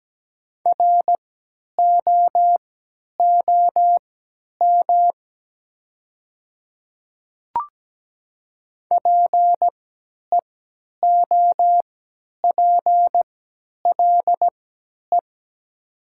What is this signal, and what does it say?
Morse code, heard as a keyed tone, about 17 wpm, spaced as for 10 wpm, 700 Hz. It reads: ROOM PEOPLE
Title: 3x Character Spacing - Top 500 Words - 17wpm